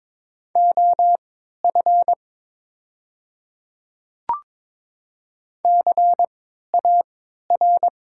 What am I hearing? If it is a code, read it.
Morse code, heard as a keyed tone, about 22 wpm, spaced as for 13 wpm, 700 Hz. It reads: OF CAR
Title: 3x Character Spacing - Top 300 Words - 22wpm